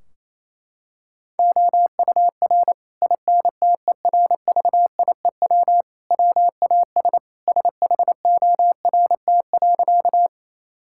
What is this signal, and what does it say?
Morse code, heard as a keyed tone, about 28 wpm, 700 Hz. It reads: OUR INTERVIEW WAS SHORT.
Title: Sentences from Top 1600 Words 28wpm